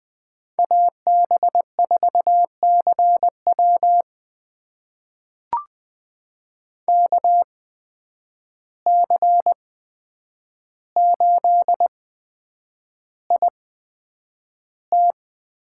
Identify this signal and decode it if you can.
Morse code, heard as a keyed tone, about 20 wpm, spaced as for 9 wpm, 700 Hz. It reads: AB4CW KC8IT
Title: USA Call Signs - 5 Characters - 8x Character Spacing - 20wpm